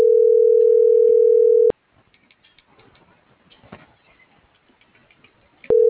Ringback tone (North America), 440 + 480 Hz.
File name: CENTRAL GENERAL / out ricardo.diaz